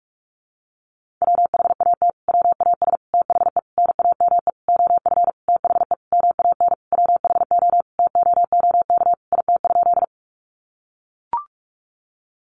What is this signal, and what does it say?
Morse code, heard as a keyed tone, about 44 wpm, 700 Hz. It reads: WHAT WAS THE NAME OF THE MAN WHO TOOK IT?
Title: Sentences from Top 300 Words 45wpm